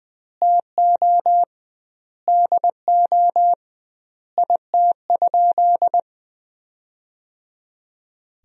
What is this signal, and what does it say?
Morse code, heard as a keyed tone, about 20 wpm, 700 Hz. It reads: TO DO IT?
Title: 2x Word-Spacing - Sentences from Top 200 Words - 20wpm